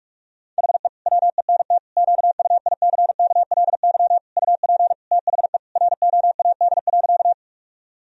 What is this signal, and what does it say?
Morse code, heard as a keyed tone, about 46 wpm, 700 Hz. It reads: HE WENT QUICKLY UP THE ROAD.
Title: Sentences from Top 1000 Words - Review Part 2 - 45wpm